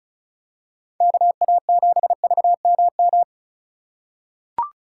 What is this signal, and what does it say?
Morse code, heard as a keyed tone, about 35 wpm, 700 Hz. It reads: KA7VMM
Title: USA Call Signs - 6 Characters - 3x Character Spacing - 35wpm